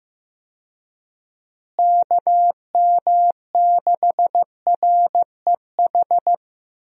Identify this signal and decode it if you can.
Morse code, heard as a keyed tone, about 15 wpm, 700 Hz. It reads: KM6REH
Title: USA Call Signs - 6 Characters - Repeated 2x - 15wpm